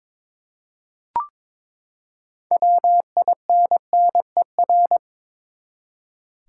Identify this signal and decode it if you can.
Morse code, heard as a keyed tone, about 21 wpm, 700 Hz. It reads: WINNER
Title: Six Letter Words - 22wpm